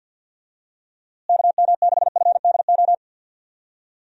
Morse code, heard as a keyed tone, about 48 wpm, 700 Hz. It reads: KM6FDO
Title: USA Call Signs - 6 Characters - Repeated 2x - 50wpm